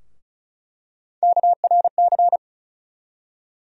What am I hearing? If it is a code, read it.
Morse code, heard as a keyed tone, about 36 wpm, 700 Hz. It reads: KRC
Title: USA Call Sign Postfixes - 3 Letters 35wpm